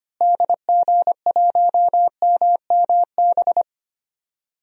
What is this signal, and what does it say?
Morse code, heard as a keyed tone, about 25 wpm, 700 Hz. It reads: DG1MMB
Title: DXCC-100 Call Signs - 6 Characters - Repeated 2x - 25wpm